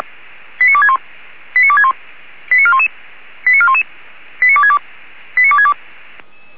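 Five-tone selective call (ZVEI), heard as quick runs of five tones, tones of about 70 ms. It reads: 89261 89261 89411 89411 89262 89262
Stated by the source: ZVEI five-tone call received by SDR.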